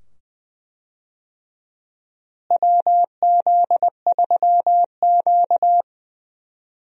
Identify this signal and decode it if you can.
Morse code, heard as a keyed tone, about 20 wpm, 700 Hz. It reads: WZ3Q
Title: USA Call Signs - 4 Characters - Repeated 2x - 20wpm